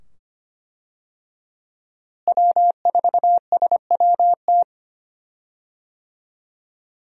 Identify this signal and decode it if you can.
Morse code, heard as a keyed tone, about 25 wpm, 700 Hz. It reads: W4SWT
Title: USA Call Signs - 5 Characters - Repeated 2x - 25wpm